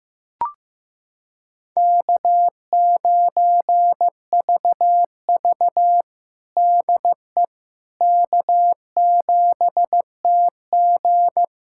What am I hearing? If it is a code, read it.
Morse code, heard as a keyed tone, about 16 wpm, 700 Hz. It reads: K9VV DE K7TG
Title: USA Callsign DE Callsign - 4 Characters - 15wpm